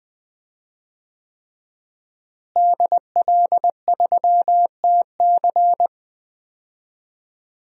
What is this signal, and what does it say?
Morse code, heard as a keyed tone, about 20 wpm, 700 Hz. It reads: DL3TC
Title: DXCC-100 Call Signs - 5 Characters - Repeated 3x - 20wpm